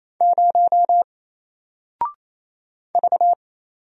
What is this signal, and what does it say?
Morse code, heard as a keyed tone, about 28 wpm, 700 Hz. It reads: 0 V
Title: Single Character (Letter:Number:Punctuation:Prosign) 28wpm